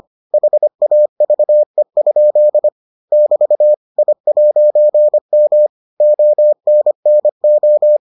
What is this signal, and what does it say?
Morse code, heard as a keyed tone, about 25 wpm, 600 Hz. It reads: HAVE? <BT> I'M ONNO